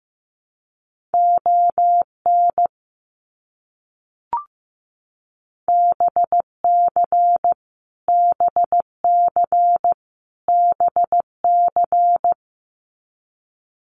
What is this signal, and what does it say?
Morse code, heard as a keyed tone, about 15 wpm, 700 Hz. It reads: ON BC BC BC
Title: Canadian Provinces and Territories - Repeated 3x - 15wpm